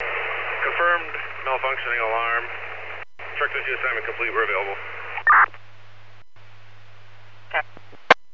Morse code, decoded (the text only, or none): none